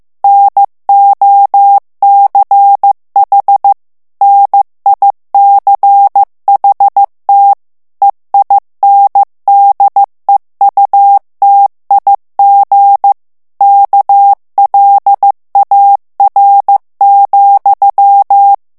Morse code, heard as a keyed tone, about 15 wpm, 800 Hz. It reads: NOCH NICHT EINDEUTIG KLAR,
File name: morse mp3 / ALLEMAND texte clair 1 vitesse 14 mots minute